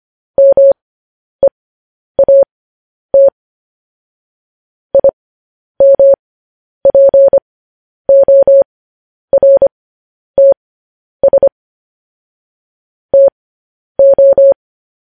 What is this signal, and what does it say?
Morse code, heard as a keyed tone, about 25 wpm, spaced as for 10 wpm, 550 Hz. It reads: MEAT IMPORTS TO
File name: podcast / News headlines-1x-25x10